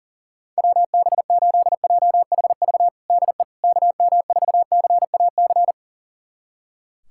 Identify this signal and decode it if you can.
Morse code, heard as a keyed tone, about 40 wpm, 700 Hz. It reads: WB8JHV DE KM4CAC
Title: USA Callsign DE Callsign 40wpm